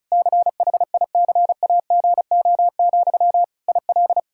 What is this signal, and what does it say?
Morse code, heard as a keyed tone, about 34 wpm, 700 Hz. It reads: CHICAGO, IL
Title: Practice QSO - Exchange 1 - Beginning - 35wpm